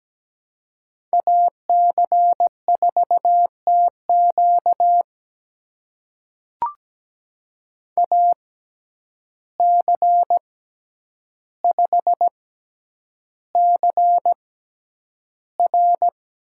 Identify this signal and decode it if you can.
Morse code, heard as a keyed tone, about 17 wpm, spaced as for 9 wpm, 700 Hz. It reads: AC4TQ AC5CR
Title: USA Call Signs - 5 Characters - 6x Character Spacing - 17wpm